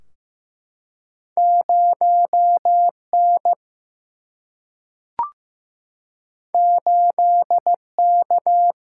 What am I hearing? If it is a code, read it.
Morse code, heard as a keyed tone, about 15 wpm, 700 Hz. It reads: 0 N 8 K
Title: Single-Letter Single-Number 15wpm